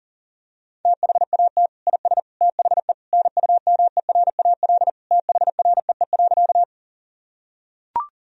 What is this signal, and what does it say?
Morse code, heard as a keyed tone, about 37 wpm, 700 Hz. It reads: THAT IS THE NUMERAL THREE.
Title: Sentences from Top 1000 Words - Review Part 2 - 40wpm